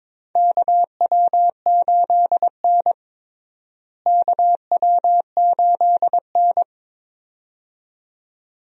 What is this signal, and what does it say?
Morse code, heard as a keyed tone, about 22 wpm, 700 Hz. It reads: KW8N KW8N
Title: USA Call Signs - 4 Characters - Repeated 2x - 22wpm